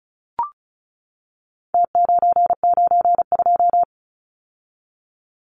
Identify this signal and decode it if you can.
Morse code, heard as a keyed tone, about 34 wpm, 700 Hz. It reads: T992
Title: Numbers - Contest Style - 4-Digit 35wpm